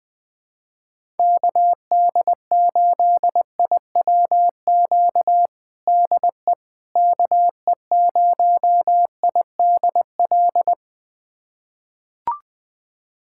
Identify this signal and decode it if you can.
Morse code, heard as a keyed tone, about 20 wpm, 700 Hz. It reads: KD8IWQ DE KE0IDL
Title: USA Callsign DE Callsign - 6 Characters - 20wpm